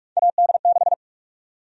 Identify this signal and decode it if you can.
Morse code, heard as a keyed tone, about 46 wpm, 700 Hz. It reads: AD6
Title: USA Call Sign Prefixes - 45wpm